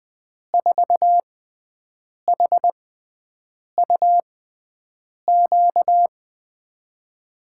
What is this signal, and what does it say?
Morse code, heard as a keyed tone, about 20 wpm, spaced as for 7 wpm, 700 Hz. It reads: 4HUQ
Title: USA Call Signs - 6 Characters - 6x Character Spacing - 20wpm